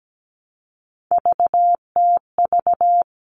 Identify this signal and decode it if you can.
Morse code, heard as a keyed tone, about 17 wpm, 700 Hz. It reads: VTV
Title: All Cut Numbers - 3-Digit 17wpm